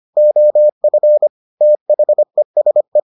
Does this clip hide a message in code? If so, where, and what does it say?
Yes: Morse code from 0.2 to 3.0 s, heard as a keyed tone, about 25 wpm, 600 Hz. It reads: OF THESE